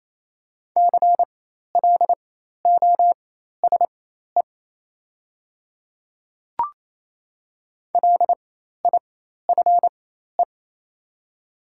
Morse code, heard as a keyed tone, about 28 wpm, spaced as for 13 wpm, 700 Hz. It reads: CLOSE LIFE
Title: 4x Character Spacing - Top 300 Words - 28wpm